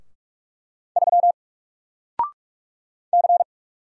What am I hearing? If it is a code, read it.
Morse code, heard as a keyed tone, about 44 wpm, 700 Hz. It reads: 3 C
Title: Single Character (Letter:Number:Punctuation:Prosign) Alt 45wpm